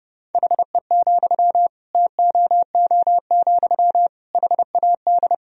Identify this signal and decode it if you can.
Morse code, heard as a keyed tone, about 30 wpm, 700 Hz. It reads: HE, TOO, HAD
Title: Sentences from Top 1000 Words - Review Part 3 - 30wpm